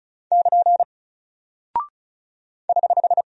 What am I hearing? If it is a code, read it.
Morse code, heard as a keyed tone, about 36 wpm, 700 Hz. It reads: <KN> <HH>
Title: Single Character (Letter:Number:Punctuation:Prosign) Alt 35wpm